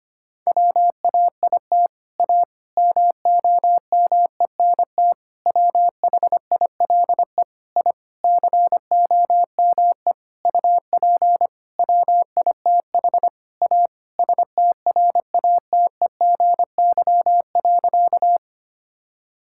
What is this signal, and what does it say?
Morse code, heard as a keyed tone, about 25 wpm, 700 Hz. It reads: WAIT A MOMENT WHILE I COME UP WITH A STRATEGY.
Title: Sentences from Top 1000 Words - Review Part 5 - 25wpm